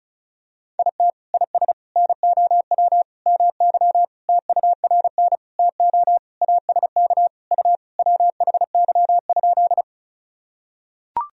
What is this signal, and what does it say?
Morse code, heard as a keyed tone, about 34 wpm, 700 Hz. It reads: IT IS NOW MY TURN TO ASK U WHY?
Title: Sentences from Top 200 Words and Abbreviations 35wpm